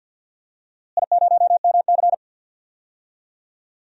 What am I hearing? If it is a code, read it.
Morse code, heard as a keyed tone, about 48 wpm, 700 Hz. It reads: I0MC
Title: DXCC-100 Call Signs - 4 Characters - Repeated 3x - 50wpm